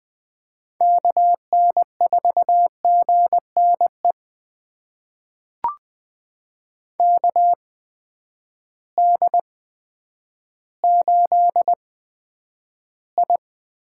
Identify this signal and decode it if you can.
Morse code, heard as a keyed tone, about 21 wpm, spaced as for 10 wpm, 700 Hz. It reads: KN4GNE KD8I
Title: USA Call Signs - 8x Character Spacing - 20wpm